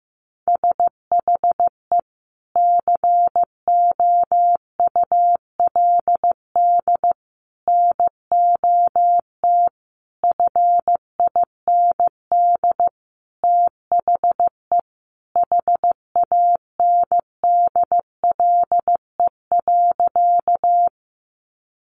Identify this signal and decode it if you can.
Morse code, heard as a keyed tone, about 15 wpm, 700 Hz. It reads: SHE COULD NOT FIND THE HANDLE.
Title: Sentences from Top 1100 Words 15wpm